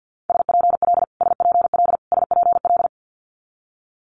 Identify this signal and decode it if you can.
Morse code, heard as a keyed tone, about 50 wpm, 690 Hz. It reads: SPL SPL SPL
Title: Multi-letter Phonemes Repeated 3x 50wpm